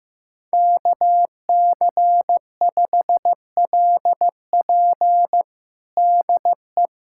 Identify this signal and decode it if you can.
Morse code, heard as a keyed tone, about 15 wpm, 700 Hz. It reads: KC5LP DE
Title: USA Callsign DE Callsign - 5 Characters - 15wpm